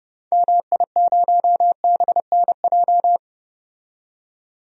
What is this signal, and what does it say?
Morse code, heard as a keyed tone, about 32 wpm, 700 Hz. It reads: MI0BNJ
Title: DXCC-100 Call Signs - 6 Characters - 30wpm